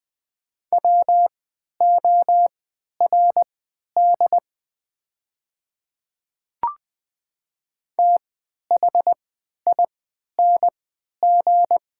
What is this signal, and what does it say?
Morse code, heard as a keyed tone, about 20 wpm, spaced as for 11 wpm, 690 Hz. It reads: WORD THING
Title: 3x Character Spacing - Top 100 Words - 20wpm